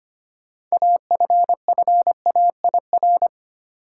Morse code, heard as a keyed tone, about 25 wpm, 700 Hz. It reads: AFFAIR
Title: Six Letter Words - 25wpm